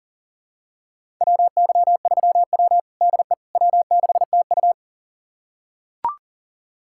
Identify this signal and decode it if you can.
Morse code, heard as a keyed tone, about 40 wpm, 700 Hz. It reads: WY3W DE W6TU
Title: USA Callsign DE Callsign - 4 Characters - 40wpm